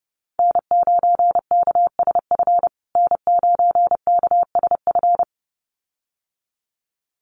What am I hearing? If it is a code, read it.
Morse code, heard as a keyed tone, about 30 wpm, 700 Hz. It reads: N9KSF N9KSF
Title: USA Call Signs Repeated 2x 30wpm